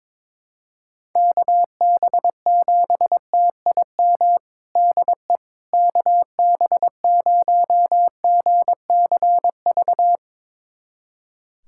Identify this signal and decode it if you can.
Morse code, heard as a keyed tone, about 22 wpm, 700 Hz. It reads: KB7TIM DE KB0GCV
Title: USA Callsign DE Callsign - 6 Characters - 22wpm